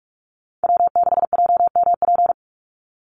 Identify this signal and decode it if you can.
Morse code, heard as a keyed tone, about 46 wpm, 700 Hz. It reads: W6JMP
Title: NA SOTA Chaser Callsigns 45wpm